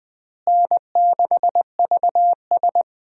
Morse code, heard as a keyed tone, about 20 wpm, 700 Hz. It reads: N6VS
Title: USA Call Signs - 4 Characters - Repeated 3x - 20wpm